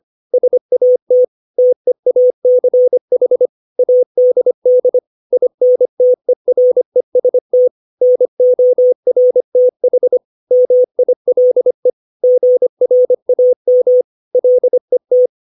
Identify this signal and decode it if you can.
Morse code, heard as a keyed tone, about 25 wpm, 500 Hz. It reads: SAT TEACH ADD INTEREST NORTH MILE GRAM LET
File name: 632 words / 632 Words 25WPM